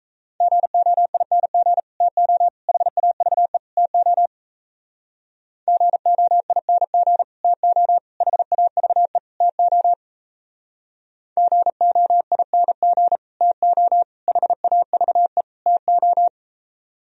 Morse code, heard as a keyed tone, about 37 wpm, 700 Hz. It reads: GOING TO HAVE TO GOING TO HAVE TO GOING TO HAVE TO